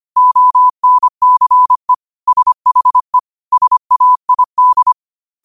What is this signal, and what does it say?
Morse code, heard as a keyed tone, about 25 wpm, 1000 Hz. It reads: ONCE SHE SAID
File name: qotdfiles / 25wpm